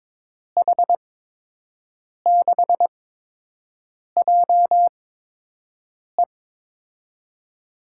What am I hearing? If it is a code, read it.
Morse code, heard as a keyed tone, about 21 wpm, spaced as for 6 wpm, 700 Hz. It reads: H6JE
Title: USA Call Signs - 8x Character Spacing - 22wpm